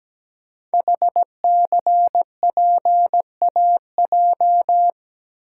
Morse code, heard as a keyed tone, about 17 wpm, 700 Hz. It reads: HCPAJ
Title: Code Groups - 5 Letters - 17wpm